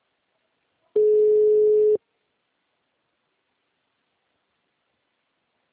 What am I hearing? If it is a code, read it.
Ringback tone (Europe, 425 Hz).